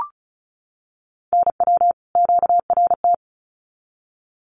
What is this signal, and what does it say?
Morse code, heard as a keyed tone, about 34 wpm, 700 Hz. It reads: NW QRT
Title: Common QSO Phrases 35wpm